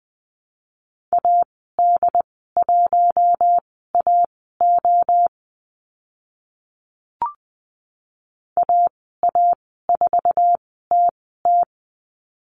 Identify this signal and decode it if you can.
Morse code, heard as a keyed tone, about 20 wpm, spaced as for 14 wpm, 700 Hz. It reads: AD1AO AA4TT
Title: 082 Lesson 14g - Number 2 - Call Signs 20wpm